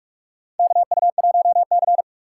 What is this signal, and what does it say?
Morse code, heard as a keyed tone, about 43 wpm, 700 Hz. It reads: KU1C
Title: USA Call Signs - 4x Character Spacing - 45wpm